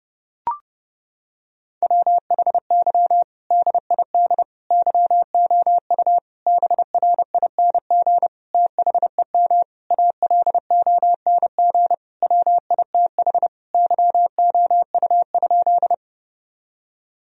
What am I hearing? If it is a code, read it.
Morse code, heard as a keyed tone, about 29 wpm, 700 Hz. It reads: WHY DID YOU BRING THEM ALONG WITH YOU?
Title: Sentences from Top 1000 Words - Review Part 3 - 30wpm